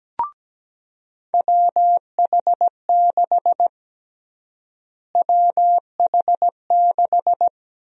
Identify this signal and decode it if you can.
Morse code, heard as a keyed tone, about 17 wpm, 700 Hz. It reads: WH6 WH6
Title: USA Call Sign Prefixes - Repeated 2x - 17wpm